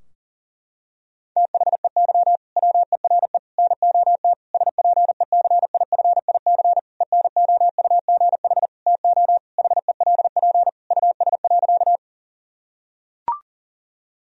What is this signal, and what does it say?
Morse code, heard as a keyed tone, about 40 wpm, 700 Hz. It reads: THEY WERE NOT SPECIFIC ENOUGH TO HELP US.
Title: Sentences from Top 1000 Words - Review Part 3 - 40wpm